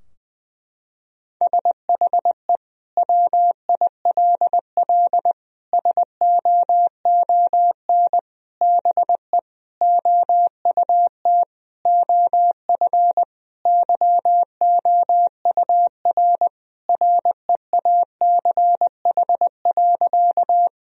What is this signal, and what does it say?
Morse code, heard as a keyed tone, about 21 wpm, 700 Hz. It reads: SHE WILL SOON BE OUT OF YOUR REACH.
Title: Sentences from Top 1000 Words - Review Part 2 - 20wpm